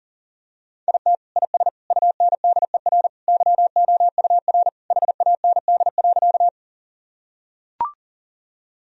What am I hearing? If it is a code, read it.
Morse code, heard as a keyed tone, about 40 wpm, 700 Hz. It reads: IT IS UNDER YOUR HAND.
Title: Sentences from Top 1000 Words - Review Part 1 - 40wpm